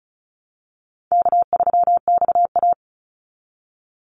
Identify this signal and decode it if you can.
Morse code, heard as a keyed tone, about 34 wpm, 700 Hz. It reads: K3XA
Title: Easy Call Signs Repeated 2x - 35wpm